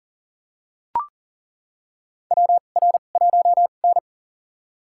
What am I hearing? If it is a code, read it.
Morse code, heard as a keyed tone, about 39 wpm, spaced as for 28 wpm, 690 Hz. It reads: WR1N
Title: USA Call Signs - 4 Characters - 2x Character Spacing - 40wpm